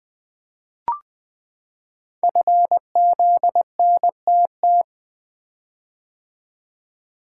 Morse code, heard as a keyed tone, about 20 wpm, 700 Hz. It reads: FZNTT
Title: Code Groups - 5 Letters - 20wpm